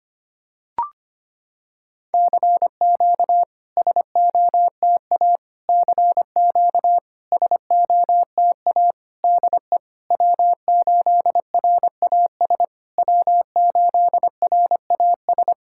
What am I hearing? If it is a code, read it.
Morse code, heard as a keyed tone, about 25 wpm, 700 Hz. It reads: CQ SOTA CQ SOTA DE W8RAS W8RAS
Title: SOTA QSO Practice 25wpm